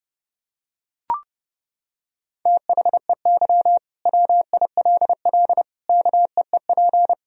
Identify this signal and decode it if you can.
Morse code, heard as a keyed tone, about 30 wpm, 700 Hz. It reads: THEY WILL KEEP
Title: Sentences from Top 1000 Words - Review Part 3 - 30wpm